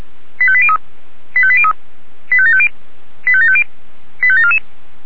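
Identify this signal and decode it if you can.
Five-tone selective call (ZVEI), heard as quick runs of five tones, tones of about 70 ms. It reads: 85803 85803 85750 85750 85744
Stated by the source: ZVEI five-tone selective call 85827 received by SDR.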